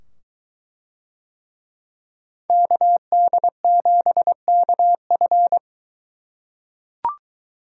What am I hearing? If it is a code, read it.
Morse code, heard as a keyed tone, about 23 wpm, 700 Hz. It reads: KD7KF